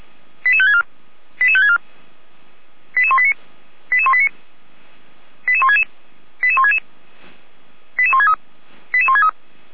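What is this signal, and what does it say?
Five-tone selective call (ZVEI), heard as quick runs of five tones, tones of about 70 ms. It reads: 88564 88564 88189 88189 88166 88166 88163 88163
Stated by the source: ZVEI five-tone selective call, code 88169, received on an SDR by the Feuerwehr Orlamünde fire brigade.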